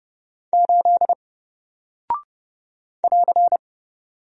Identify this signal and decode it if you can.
Morse code, heard as a keyed tone, about 29 wpm, 700 Hz. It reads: 8 <AR>
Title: Single Character (Letter:Number:Punctuation:Prosign) 30wpm